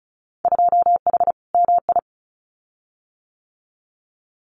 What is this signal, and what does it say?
Morse code, heard as a keyed tone, about 36 wpm, 700 Hz. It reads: 2H MI
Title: Winter Field Day Exchanges Encore 35wpm